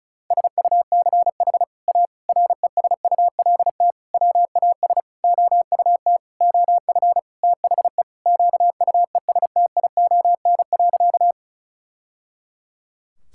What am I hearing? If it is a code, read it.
Morse code, heard as a keyed tone, about 36 wpm, 700 Hz. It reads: SUCH A RESULT WAS OUT OF THE QUESTION.
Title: Sentences from Top 1000 Words - Review Part 5 - 35wpm